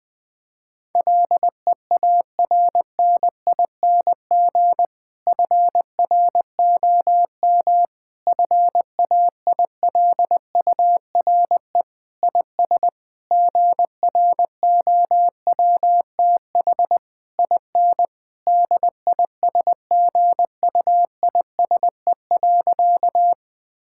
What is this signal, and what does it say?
Morse code, heard as a keyed tone, about 21 wpm, 700 Hz. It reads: LEARNING FROM FAILURE IS GROWTH IN DISGUISE.
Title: Difficult Sentences with Hints - 20wpm